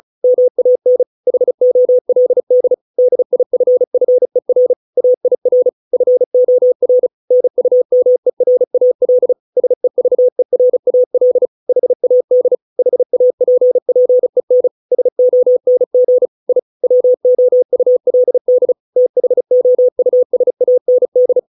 Morse code, heard as a keyed tone, about 34 wpm, 500 Hz. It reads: MAN HOLD DIFFER AIR FOR NUMERAL SEVERAL HAD HAPPEN SONG I WOULD THOUSAND